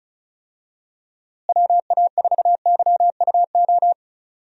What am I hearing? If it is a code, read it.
Morse code, heard as a keyed tone, about 36 wpm, 700 Hz. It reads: WA4YUO
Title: USA Call Signs - 6 Characters - Repeated 2x - 35wpm